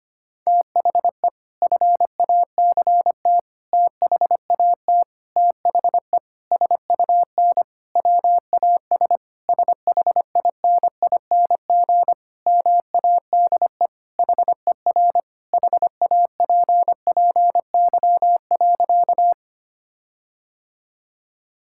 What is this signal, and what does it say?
Morse code, heard as a keyed tone, about 25 wpm, 700 Hz. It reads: THE FACT THAT THE SUN WAS SHINING MADE HER HAPPY.
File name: practice / 25wpm / Sentences with 4-Word Phrases - 25wpm